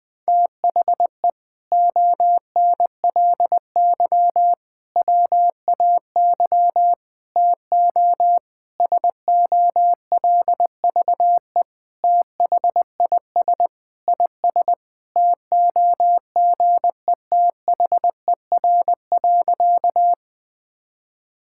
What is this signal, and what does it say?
Morse code, heard as a keyed tone, about 20 wpm, 700 Hz. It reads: THE ONLY WAY TO SOLVE THIS IS TOGETHER.
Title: Sentences with 4-Word Phrases - 20wpm